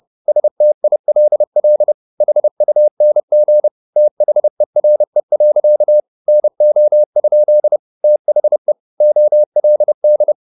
Morse code, heard as a keyed tone, about 30 wpm, 600 Hz. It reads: STILL HUNG THERE. NO? THE OLD